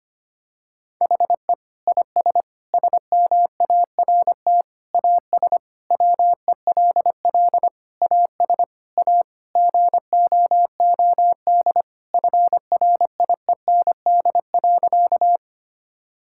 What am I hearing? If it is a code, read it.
Morse code, heard as a keyed tone, about 25 wpm, 700 Hz. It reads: HE IS SMART AS WELL AS A GOOD FRIEND.
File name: practice / 25wpm / Sentences with 4-Word Phrases - 25wpm